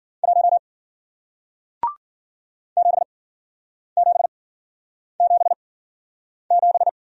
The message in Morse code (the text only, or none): 1 7777